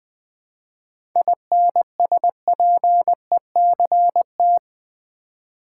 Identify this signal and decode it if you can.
Morse code, heard as a keyed tone, about 20 wpm, 690 Hz. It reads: INSPECT
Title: 7 Letter Words - First 3 Letters Given - 20wpm